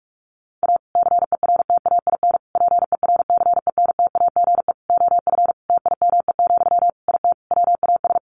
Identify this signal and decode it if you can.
Morse code, heard as a keyed tone, about 44 wpm, 700 Hz. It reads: A CERTAIN PERCENTAGE OF TIME, IT WAS
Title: Sentences from Top 1300 Words 45wpm